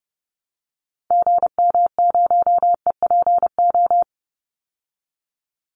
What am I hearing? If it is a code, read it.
Morse code, heard as a keyed tone, about 29 wpm, 700 Hz. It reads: GM0EPO
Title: Call Signs Repeated 2x - 30wpm